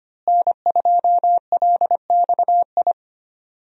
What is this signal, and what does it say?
Morse code, heard as a keyed tone, about 25 wpm, 700 Hz. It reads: N2LXI
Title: USA Call Signs - 5 Characters - Repeated 3x - 25wpm